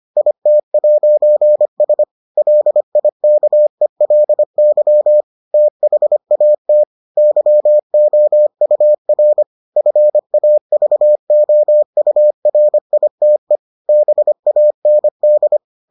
Morse code, heard as a keyed tone, about 25 wpm, 600 Hz. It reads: IT'S LIKELY THAT YOUR FAVOURITE BAND